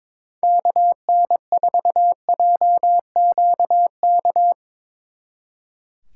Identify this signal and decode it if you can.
Morse code, heard as a keyed tone, about 24 wpm, 700 Hz. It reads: KN4JQK
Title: Speed Racing - USA 2x3 Call Signs 22wpm